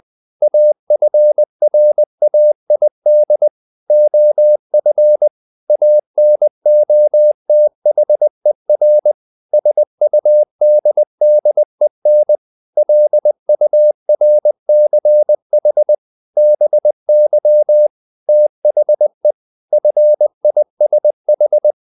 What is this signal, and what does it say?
Morse code, heard as a keyed tone, about 20 wpm, 600 Hz. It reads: AFRAID OF ANOTHER SUDDEN LURCH BY THE FISH
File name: the old man and the sea / the old man and the sea part 1 20wpm